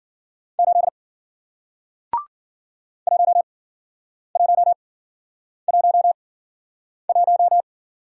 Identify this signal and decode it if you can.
Morse code, heard as a keyed tone, about 53 wpm, spaced as for 9 wpm, 700 Hz. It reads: 8 1111